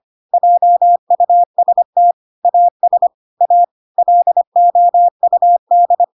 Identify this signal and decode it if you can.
Morse code, heard as a keyed tone, about 25 wpm, 700 Hz. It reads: JUST AS A LOUD